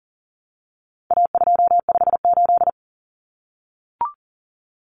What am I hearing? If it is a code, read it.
Morse code, heard as a keyed tone, about 40 wpm, 700 Hz. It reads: A258
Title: Common Cut Numbers - 4-Digit 40wpm